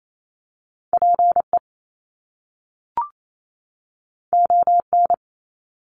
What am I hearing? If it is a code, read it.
Morse code, heard as a keyed tone, about 28 wpm, 700 Hz. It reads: PE ON
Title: Canadian Provinces and Territories 28wpm